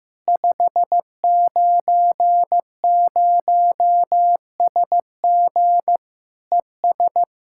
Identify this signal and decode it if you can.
Morse code, heard as a keyed tone, about 15 wpm, 700 Hz. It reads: 590SG ES
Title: K1USN SST QSO Practice 15wpm